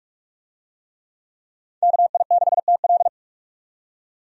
Morse code, heard as a keyed tone, about 41 wpm, 700 Hz. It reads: KI6TL
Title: USA Call Signs - 5 Characters - 45wpm